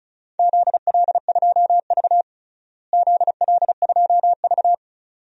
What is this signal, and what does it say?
Morse code, heard as a keyed tone, about 36 wpm, 700 Hz. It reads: ZL2V ZL2V
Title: DXCC-100 Call Signs - 4 Characters - Repeated 2x - 35wpm